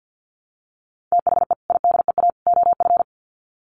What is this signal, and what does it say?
Morse code, heard as a keyed tone, about 50 wpm, 700 Hz. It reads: THE IDEA OF
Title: Sets of 3 Words 50wpm